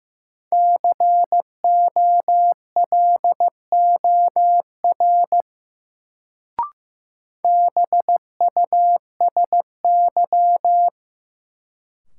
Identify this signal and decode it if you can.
Morse code, heard as a keyed tone, about 15 wpm, 700 Hz. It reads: COLOR BUSY
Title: Sentences from Top 1000 Words - Review Part 2 - New Word Drill - 15wpm